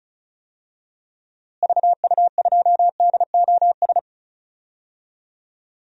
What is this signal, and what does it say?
Morse code, heard as a keyed tone, about 36 wpm, 700 Hz. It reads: VU2DOS
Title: DXCC-100 Call Signs - 6 Characters - Repeated 2x - 35wpm